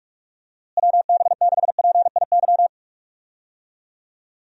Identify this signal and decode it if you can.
Morse code, heard as a keyed tone, about 46 wpm, 700 Hz. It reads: WB6PIY
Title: USA Call Signs - 6 Characters - Repeated 2x - 45wpm